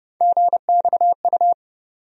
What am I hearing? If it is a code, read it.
Morse code, heard as a keyed tone, about 31 wpm, 700 Hz. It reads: GXU
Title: USA Call Sign Postfixes - 3 Letters 30wpm